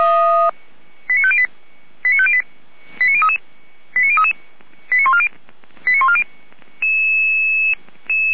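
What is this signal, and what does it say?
Five-tone selective call (ZVEI), heard as quick runs of five tones, tones of about 70 ms. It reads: 89508 89508 89033 89033 89140 89140
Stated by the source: ZVEI five-tone selective call received on an SDR.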